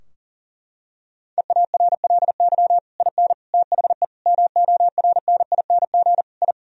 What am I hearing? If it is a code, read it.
Morse code, heard as a keyed tone, about 40 wpm, 700 Hz. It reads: EARLY IN THE MORNING I
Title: Sentences from Top 1000 Words - Review Part 2 - 40wpm